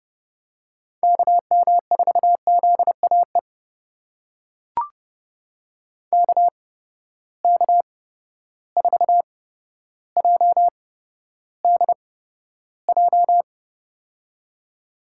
Morse code, heard as a keyed tone, about 30 wpm, spaced as for 13 wpm, 700 Hz. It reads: KM4ZAE KK4JDJ
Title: USA Call Signs - 8x Character Spacing - 30wpm